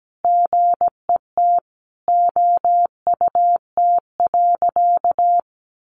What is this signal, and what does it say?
Morse code, heard as a keyed tone, about 17 wpm, 700 Hz. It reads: GET OUT.
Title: A Sound Of Thunder 17wpm